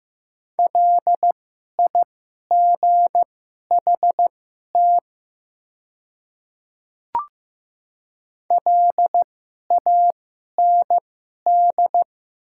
Morse code, heard as a keyed tone, about 15 wpm, spaced as for 11 wpm, 700 Hz. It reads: LIGHT LAND
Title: 2x Character Spacing - Top 200 Words - 15wpm